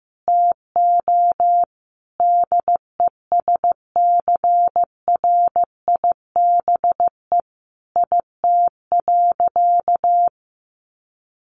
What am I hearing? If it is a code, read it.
Morse code, heard as a keyed tone, about 15 wpm, 700 Hz. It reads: TO DESCRIBE IT.
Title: Sentences from Top 1100 Words 15wpm